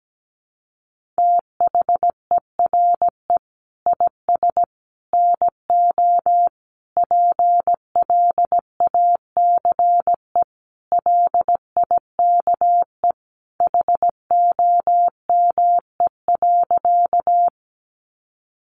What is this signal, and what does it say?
Morse code, heard as a keyed tone, about 17 wpm, 700 Hz. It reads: THERE IS NO PLACE LIKE HOME.
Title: Sentences from Top 200 Words 17wpm